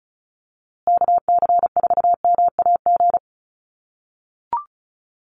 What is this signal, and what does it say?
Morse code, heard as a keyed tone, about 34 wpm, 700 Hz. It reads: KC4MAG
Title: USA Call Signs Repeated 3x 35wpm